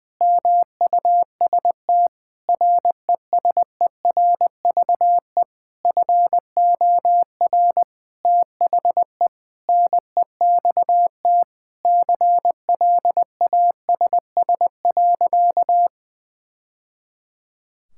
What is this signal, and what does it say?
Morse code, heard as a keyed tone, about 21 wpm, 700 Hz. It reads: MUST RESERVE FOR THE NEXT CLASS.
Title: Sentences from Top 1000 Words - Review Part 6 - 20wpm